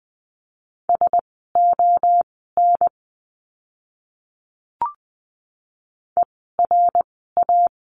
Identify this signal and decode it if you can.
Morse code, heard as a keyed tone, about 20 wpm, spaced as for 15 wpm, 700 Hz. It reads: SON ERA
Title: 054 Lesson 10f - Letter r - Words 20wpm